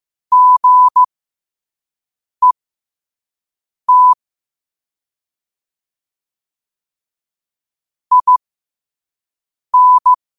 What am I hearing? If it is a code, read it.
Morse code, heard as a keyed tone, about 15 wpm, spaced as for 5 wpm, 1000 Hz. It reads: GET IN
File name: qotdfiles / 5wpm